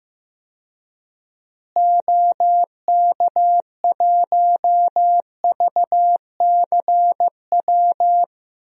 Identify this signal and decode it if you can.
Morse code, heard as a keyed tone, about 15 wpm, 700 Hz. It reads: OK1VCW
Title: DXCC-100 Call Signs - 6 Characters - Repeated 3x - 15wpm